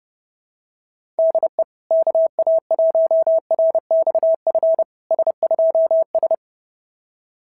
Morse code, heard as a keyed tone, about 30 wpm, 650 Hz. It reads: DE KA1RXF S2S
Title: SOTA QSO Practice 30wpm